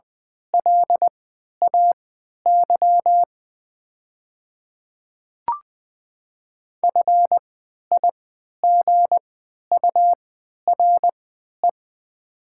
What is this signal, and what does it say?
Morse code, heard as a keyed tone, about 20 wpm, spaced as for 11 wpm, 700 Hz. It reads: LAY FIGURE
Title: 3x Character Spacing - Top 500 Words - 20wpm